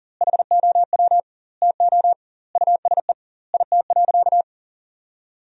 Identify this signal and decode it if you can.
Morse code, heard as a keyed tone, about 40 wpm, 700 Hz. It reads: HOW TO USE IT.
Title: 2x Word-Spacing - Sentences from Top 500 Words - 40wpm